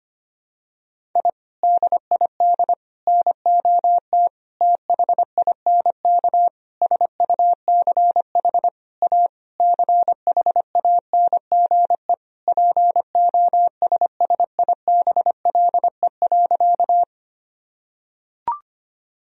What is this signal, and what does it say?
Morse code, heard as a keyed tone, about 25 wpm, 700 Hz. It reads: I DID NOT THINK SUCH A CHANGE POSSIBLE.
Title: Sentences from Top 1000 Words - Review Part 1 - 25wpm